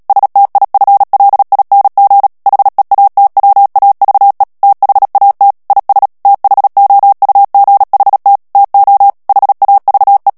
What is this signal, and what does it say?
Morse code, heard as a keyed tone, about 37 wpm, 800 Hz. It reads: STIFLING HEATWAVE THAT IS THOUGHT TO HAVE
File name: morse mp3 / ANGLAIS texte clair 2 vitesse 35 mots minute